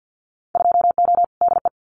Morse code, heard as a keyed tone, about 50 wpm, 700 Hz. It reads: 2O DE
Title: Winter Field Day Exchanges Encore 50wpm